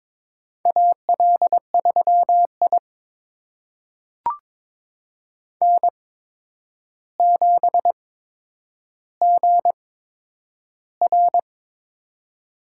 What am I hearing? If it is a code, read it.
Morse code, heard as a keyed tone, about 22 wpm, spaced as for 9 wpm, 700 Hz. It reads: AL3I N7GR
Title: USA Call Signs - 4 Characters - 8x Character Spacing - 22wpm